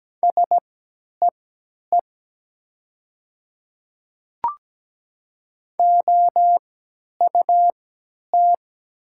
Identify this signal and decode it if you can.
Morse code, heard as a keyed tone, about 17 wpm, spaced as for 10 wpm, 700 Hz. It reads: SEE OUT
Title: 3x Character Spacing - Top 300 Words - 17wpm